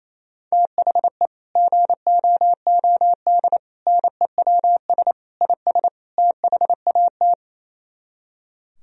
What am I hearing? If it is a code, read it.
Morse code, heard as a keyed tone, about 37 wpm, 700 Hz. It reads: THE GOOD NEWS IS THAT